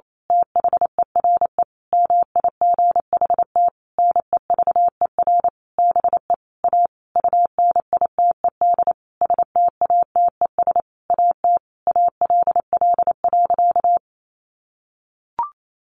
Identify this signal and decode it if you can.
Morse code, heard as a keyed tone, about 28 wpm, 700 Hz. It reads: THERE MIGHT NEVER BE A UNITED STATES AT ALL.
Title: A Sound Of Thunder 28wpm